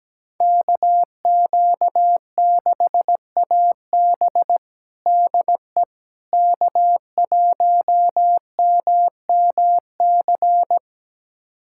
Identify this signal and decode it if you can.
Morse code, heard as a keyed tone, about 17 wpm, 700 Hz. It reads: KQ6AB DE K1MMC
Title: USA Callsign DE Callsign - 5 Characters - 17wpm